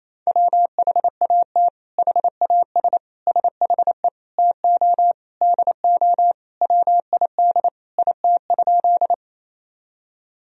Morse code, heard as a keyed tone, about 27 wpm, 700 Hz. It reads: WHAT HAS SHE TO DO WID IT?
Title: Sentences from Top 100 Words and Abbreviations 28wpm